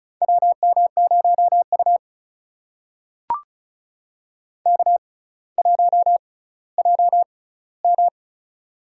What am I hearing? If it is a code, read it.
Morse code, heard as a keyed tone, about 36 wpm, spaced as for 18 wpm, 700 Hz. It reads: WM0U K1JM
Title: USA Call Signs - 4 Characters - 6x Character Spacing - 35wpm